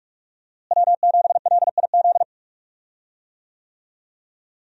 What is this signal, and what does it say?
Morse code, heard as a keyed tone, about 46 wpm, 700 Hz. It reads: W7LIZ
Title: USA Call Signs - 5 Characters - Repeated 2x - 45wpm